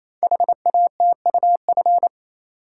Morse code, heard as a keyed tone, about 27 wpm, 700 Hz. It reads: HATUF
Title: Code Groups - 5 Letters - 28wpm